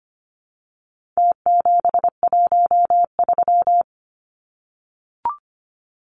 Morse code, heard as a keyed tone, about 25 wpm, 700 Hz. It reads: T713
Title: Numbers - Contest Style - 4-Digit 25wpm